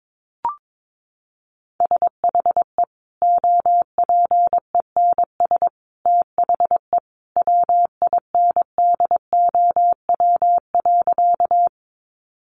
Morse code, heard as a keyed tone, about 22 wpm, 700 Hz. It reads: SHE OPENS THE WINDOW.
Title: Easy Phrases 22wpm